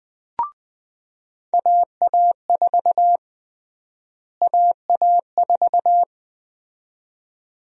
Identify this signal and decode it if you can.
Morse code, heard as a keyed tone, about 21 wpm, 700 Hz. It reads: AA4 AA4
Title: USA Call Sign Prefixes - Repeated 2x - 20wpm